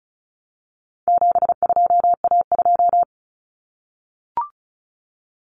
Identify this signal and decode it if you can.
Morse code, heard as a keyed tone, about 34 wpm, 700 Hz. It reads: 72A2
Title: Common Cut Numbers - 4-Digit 35wpm